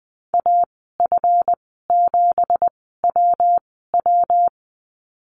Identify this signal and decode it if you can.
Morse code, heard as a keyed tone, about 20 wpm, spaced as for 15 wpm, 700 Hz. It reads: AF7WW
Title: 200 Lesson 32g - Letter k - Call Signs 20wpm